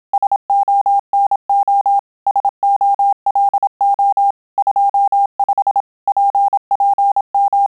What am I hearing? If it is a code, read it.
Morse code, heard as a keyed tone, about 27 wpm, 800 Hz. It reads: SONO SOLO 25 PPM